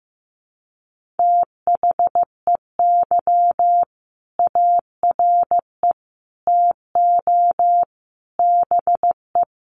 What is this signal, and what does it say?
Morse code, heard as a keyed tone, about 15 wpm, 700 Hz. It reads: THEY ARE TO BE
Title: Sentences from Top 1400 Words 15wpm